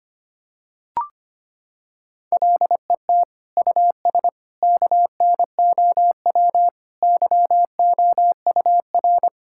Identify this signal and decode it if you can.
Morse code, heard as a keyed tone, about 25 wpm, 700 Hz. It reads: LET US KNOW YOUR
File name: practice / 25wpm / Sentences with 9-Word Phrases - 25wpm